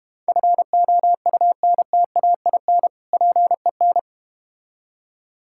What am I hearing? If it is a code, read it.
Morse code, heard as a keyed tone, about 32 wpm, 700 Hz. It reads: FOUNTAIN PEN